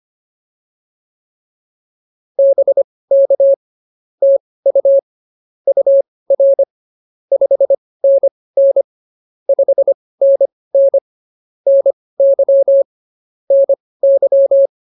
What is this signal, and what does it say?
Morse code, heard as a keyed tone, about 24 wpm, spaced as for 18 wpm, 550 Hz. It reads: BK TU UR 5NN 5NN NY NY